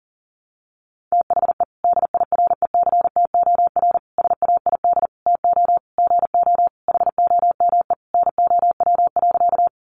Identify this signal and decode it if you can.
Morse code, heard as a keyed tone, about 40 wpm, 700 Hz. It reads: THE DIRECTOR SAID TO GO HOME NOW.
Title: Sentences from Top 1300 Words 40wpm